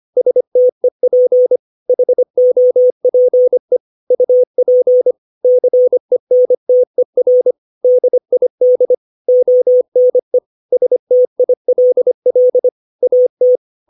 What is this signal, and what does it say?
Morse code, heard as a keyed tone, about 25 wpm, 500 Hz. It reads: STEP HOPE UP CENTER DID ONE STILL AT
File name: 632 words / 632 Words 25WPM